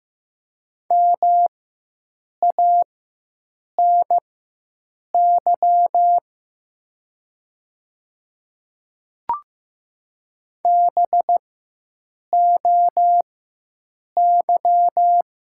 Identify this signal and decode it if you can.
Morse code, heard as a keyed tone, about 15 wpm, spaced as for 7 wpm, 700 Hz. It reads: MANY BOY
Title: 4x Character Spacing - Top 300 Words - 15wpm